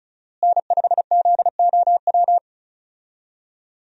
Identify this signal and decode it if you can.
Morse code, heard as a keyed tone, about 34 wpm, 700 Hz. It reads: N5ZOW
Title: USA Call Signs - 5 Characters - Repeated 2x - 35wpm